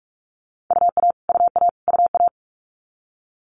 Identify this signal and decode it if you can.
Morse code, heard as a keyed tone, about 46 wpm, 700 Hz. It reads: UA UA UA
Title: Multi-letter Phonemes Repeated 3x 45wpm